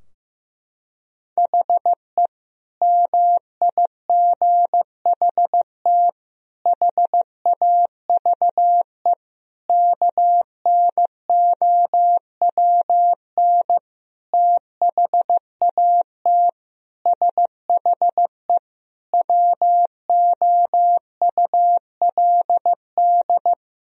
Morse code, heard as a keyed tone, about 15 wpm, 700 Hz. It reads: HE MIGHT HAVE KNOWN THAT SHE WOULD
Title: Sentences from Top 1000 Words - Review Part 3 - 15wpm